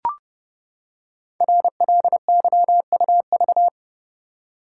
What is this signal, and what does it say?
Morse code, heard as a keyed tone, about 30 wpm, 700 Hz. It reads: RLYUV
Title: Code Groups - 5 Letters - 30wpm